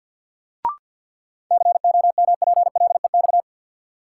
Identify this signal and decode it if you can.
Morse code, heard as a keyed tone, about 52 wpm, 700 Hz. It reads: COMPLEX
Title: Sentences from Top 900 Words - New Word Drill - 50wpm